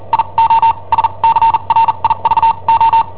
Morse code, heard as a keyed tone, about 43 wpm, spaced as for 33 wpm, 910 Hz. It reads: IO SCRIVO
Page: Cassetta esercizi preparata da I7OHP per IW5ELP